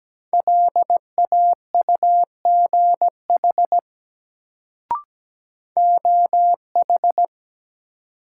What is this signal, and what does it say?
Morse code, heard as a keyed tone, about 17 wpm, 700 Hz. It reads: LAUGH OH
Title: Sentences from Top 500 Words - New Word Drill - 17wpm